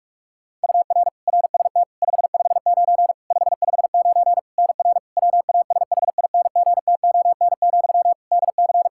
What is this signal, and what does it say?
Morse code, heard as a keyed tone, about 41 wpm, 700 Hz. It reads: UR RST 559 559 NR WASHINGTON, DC
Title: Practice QSO - Exchange 1 - Beginning - 45wpm